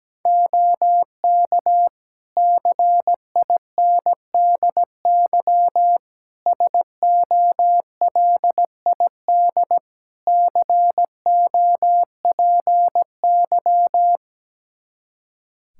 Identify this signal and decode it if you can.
Morse code, heard as a keyed tone, about 17 wpm, 700 Hz. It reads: OK CINDY SOLID COPY
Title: K1USN SST QSO Practice 17wpm